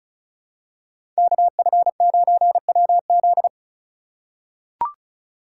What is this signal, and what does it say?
Morse code, heard as a keyed tone, about 34 wpm, 700 Hz. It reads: KF9WZ
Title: USA Call Signs - 5 Characters - 2x Character Spacing - 35wpm